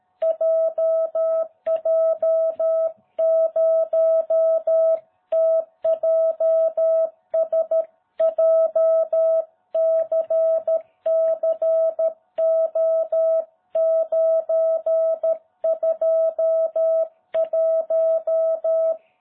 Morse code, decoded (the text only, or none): JJ0TJS JCCO921